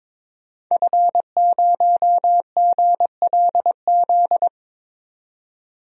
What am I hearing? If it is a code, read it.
Morse code, heard as a keyed tone, about 22 wpm, 700 Hz. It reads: F0GLZ
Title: DXCC-100 Call Signs - 5 Characters - 22wpm